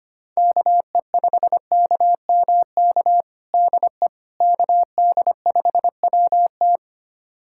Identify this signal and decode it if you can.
Morse code, heard as a keyed tone, about 25 wpm, 700 Hz. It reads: KE5KMK DE KD5WT
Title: USA Callsign DE Callsign 25wpm